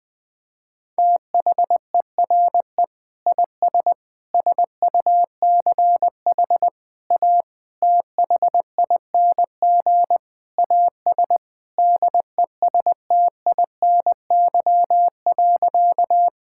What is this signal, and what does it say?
Morse code, heard as a keyed tone, about 21 wpm, 700 Hz. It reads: THERE IS SUCH A THING AS DESTINY.
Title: Sentences with 6-Word Phrases - 20wpm